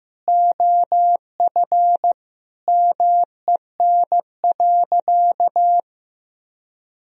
Morse code, heard as a keyed tone, about 15 wpm, 700 Hz. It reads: OF MEN.
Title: Sentences from Top 1000 Words - Review Part 1 - 15wpm